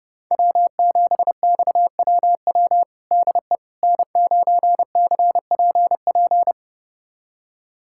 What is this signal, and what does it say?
Morse code, heard as a keyed tone, about 32 wpm, 700 Hz. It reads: W7XWW DE N9CPP
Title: USA Callsign DE Callsign - 5 Characters - 30wpm